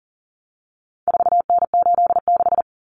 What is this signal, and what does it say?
Morse code, heard as a keyed tone, about 40 wpm, 700 Hz. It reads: 4N86
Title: Common Cut Numbers - 4-Digit 40wpm